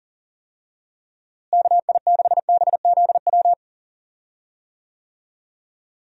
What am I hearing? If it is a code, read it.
Morse code, heard as a keyed tone, about 40 wpm, 700 Hz. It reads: KI6BZW
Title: USA Call Signs - 6 Characters - Repeated 3x - 40wpm